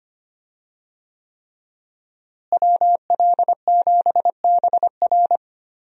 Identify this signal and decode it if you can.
Morse code, heard as a keyed tone, about 25 wpm, 700 Hz. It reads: WL7BR
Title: USA Call Signs - 5 Characters - Repeated 3x - 25wpm